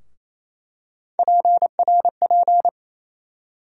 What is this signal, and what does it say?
Morse code, heard as a keyed tone, about 28 wpm, 700 Hz. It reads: PRP
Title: USA Call Sign Postfixes - 3 Letters 28wpm